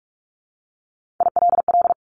Morse code, heard as a keyed tone, about 44 wpm, 700 Hz. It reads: ILL
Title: Top 1751-2000 Words 45wpm